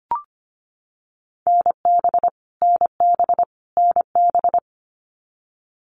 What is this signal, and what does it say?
Morse code, heard as a keyed tone, about 25 wpm, 700 Hz. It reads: NB NB NB
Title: Canadian Provinces and Territories - Repeated 3x - 25wpm